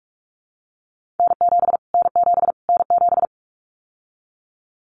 Morse code, heard as a keyed tone, about 44 wpm, 700 Hz. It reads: N7 N7 N7
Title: Callsign Prefixes for Top 25 DXCC - Repeated 3x - 45wpm